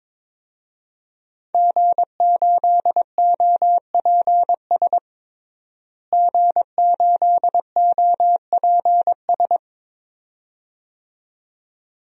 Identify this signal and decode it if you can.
Morse code, heard as a keyed tone, about 22 wpm, 700 Hz. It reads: G8OPS G8OPS
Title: DXCC-100 Call Signs - 5 Characters - Repeated 2x - 22wpm